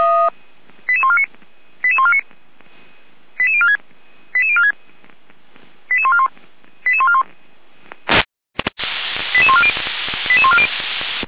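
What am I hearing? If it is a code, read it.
Five-tone selective call (ZVEI), heard as quick runs of five tones, tones of about 70 ms. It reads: 88149 88149 88046 88046 88141 88141 88140 88140
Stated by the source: ZVEI five-tone call received by SDR.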